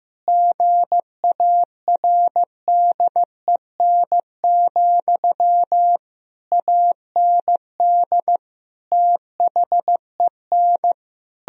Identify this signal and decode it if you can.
Morse code, heard as a keyed tone, about 15 wpm, 700 Hz. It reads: GARDEN, AND THEN
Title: Sentences from Top 1000 Words - Review Part 4 - 15wpm